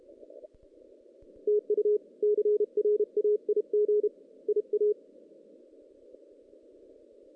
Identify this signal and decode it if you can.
Morse code, heard as a keyed tone, about 32 wpm, 420 Hz. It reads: TU CRAIG IA
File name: 2016 JAN CW NAQP / 2016 Jan naqp k5ot - 0-6